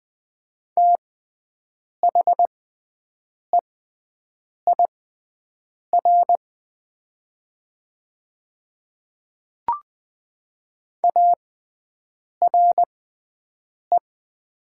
Morse code, heard as a keyed tone, about 20 wpm, spaced as for 7 wpm, 700 Hz. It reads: THEIR ARE
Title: 6x Character Spacing - Top 100 Words - 20wpm